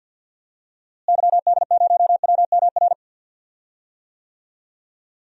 Morse code, heard as a keyed tone, about 51 wpm, 700 Hz. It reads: YD0WMR
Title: DXCC-100 Call Signs - 6 Characters - Repeated 3x - 50wpm